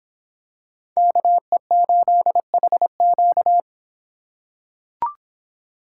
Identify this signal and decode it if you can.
Morse code, heard as a keyed tone, about 24 wpm, 700 Hz. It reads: KE8HQ